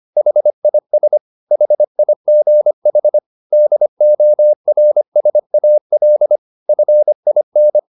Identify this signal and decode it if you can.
Morse code, heard as a keyed tone, about 25 wpm, 600 Hz. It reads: HIS HIGH DORSAL FIN